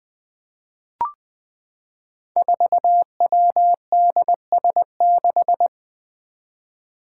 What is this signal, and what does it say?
Morse code, heard as a keyed tone, about 20 wpm, 700 Hz. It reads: 4WDS6
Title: Five Challenging Letter-Number Combinations 20wpm